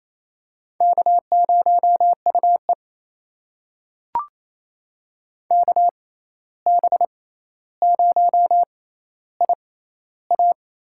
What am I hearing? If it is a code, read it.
Morse code, heard as a keyed tone, about 28 wpm, spaced as for 13 wpm, 700 Hz. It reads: K0UE KB0IA
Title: USA Call Signs - 6x Character Spacing - 28wpm